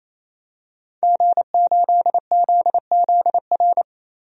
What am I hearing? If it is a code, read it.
Morse code, heard as a keyed tone, about 28 wpm, 700 Hz. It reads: G8ZZR
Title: DXCC-100 Call Signs - 5 Characters - Repeated 3x - 28wpm